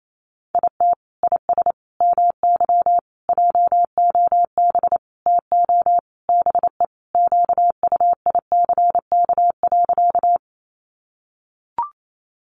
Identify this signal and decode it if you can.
Morse code, heard as a keyed tone, about 28 wpm, 700 Hz. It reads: IT IS MY JOB TO BE QUICK.
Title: Sentences from Top 600 Words 28wpm